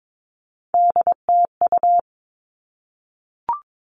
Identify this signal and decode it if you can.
Morse code, heard as a keyed tone, about 22 wpm, 700 Hz. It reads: DTU
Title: All Cut Numbers - 3-Digit 22wpm